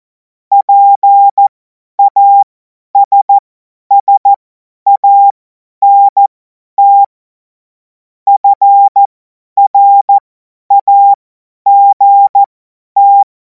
Morse code, heard as a keyed tone, about 14 wpm, spaced as for 10 wpm, 800 Hz. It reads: PASSANT FRAGT
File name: Morsekurs / Witze / Witz 07. April 2026 50 F70